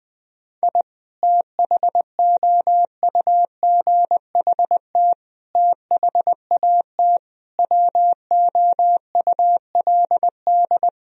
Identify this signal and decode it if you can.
Morse code, heard as a keyed tone, about 20 wpm, 700 Hz. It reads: I THOUGHT THAT WOULD
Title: Sentences from Top 1000 Words - Review Part 4 - 20wpm